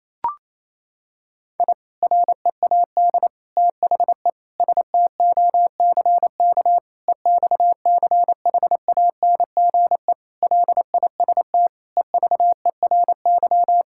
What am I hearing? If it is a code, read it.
Morse code, heard as a keyed tone, about 28 wpm, 700 Hz. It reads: I READ THE STOCK EXCHANGE LIST EVERY
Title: Sentences from Top 1000 Words - Review Part 6 - 28wpm